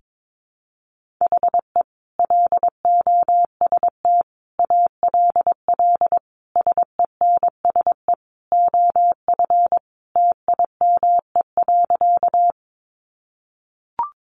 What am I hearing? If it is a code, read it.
Morse code, heard as a keyed tone, about 22 wpm, 700 Hz. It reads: HE LOST ALL SENSE OF TIME.
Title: Sentences from Top 1400 Words 22wpm